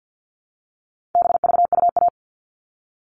Morse code, heard as a keyed tone, about 50 wpm, 700 Hz. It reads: BVUA
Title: All Cut Numbers - 4-Digit 50wpm